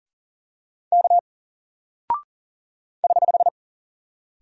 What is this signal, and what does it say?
Morse code, heard as a keyed tone, about 40 wpm, 700 Hz. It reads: K <HH>
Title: Single Character (Letter:Number:Punctuation:Prosign) Alt 40wpm